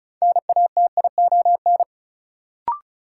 Morse code, heard as a keyed tone, about 36 wpm, 700 Hz. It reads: NATION
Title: Sets of 3 Words - New Word Drill - 35wpm